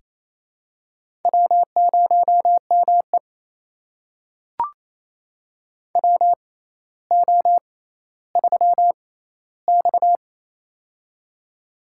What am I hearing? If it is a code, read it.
Morse code, heard as a keyed tone, about 28 wpm, spaced as for 14 wpm, 700 Hz. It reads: W0ME WO3X
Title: USA Call Signs - 4 Characters - 6x Character Spacing - 28wpm